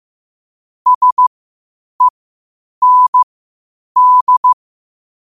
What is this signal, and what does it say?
Morse code, heard as a keyed tone, about 15 wpm, spaced as for 8 wpm, 1000 Hz. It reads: SEND
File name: qotdfiles / 8wpm